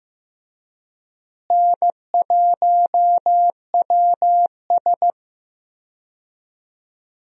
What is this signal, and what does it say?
Morse code, heard as a keyed tone, about 15 wpm, 700 Hz. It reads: N1WS
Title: USA Call Signs - 4 Characters - Repeated 3x - 15wpm